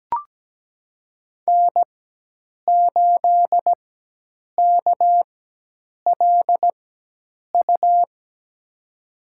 Morse code, heard as a keyed tone, about 17 wpm, spaced as for 8 wpm, 700 Hz. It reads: N8KLU
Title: USA Call Signs - 5 Characters - 4x Character Spacing - 17wpm